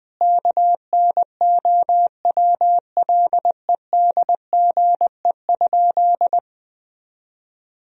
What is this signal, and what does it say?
Morse code, heard as a keyed tone, about 21 wpm, 700 Hz. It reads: KNOWLEDGE?
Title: Sentences from Top 1000 Words - Review Part 3 - 20wpm